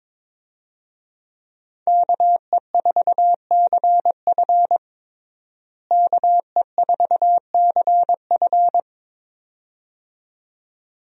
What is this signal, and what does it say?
Morse code, heard as a keyed tone, about 22 wpm, 700 Hz. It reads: KE4CF KE4CF
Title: USA Call Signs - 5 Characters - Repeated 2x - 22wpm